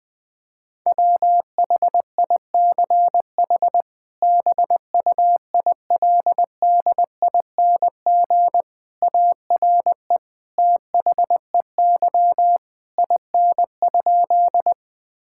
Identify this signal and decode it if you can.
Morse code, heard as a keyed tone, about 20 wpm, 700 Hz. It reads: WHICH BUILDING ARE THEY IN?
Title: Sentences from Top 1000 Words - Review Part 5 - 20wpm